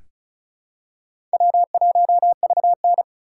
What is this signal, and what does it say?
Morse code, heard as a keyed tone, about 36 wpm, 700 Hz. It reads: W1VN
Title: USA Call Signs - 3x Character Spacing - 35wpm